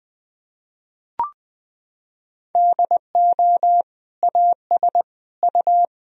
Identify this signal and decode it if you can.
Morse code, heard as a keyed tone, about 20 wpm, 700 Hz. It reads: DO AS U
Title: Sentences from Top 100 Words and Abbreviations 20wpm